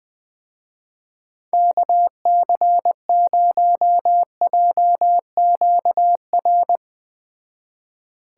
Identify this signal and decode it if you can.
Morse code, heard as a keyed tone, about 20 wpm, 700 Hz. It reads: KC0JQR
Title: Speed Racing - USA 2x3 Call Signs 17wpm